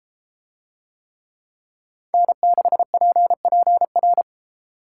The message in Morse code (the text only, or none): N6PPR